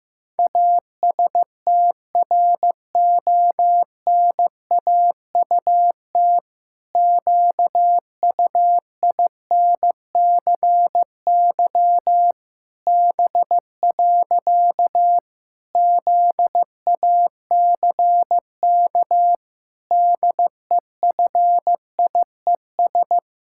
Morse code, heard as a keyed tone, about 14 wpm, 700 Hz. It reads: ASTRONAUT QUINCY B. ZACK DEFIES
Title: Pangram Challenge v2 - 15wpm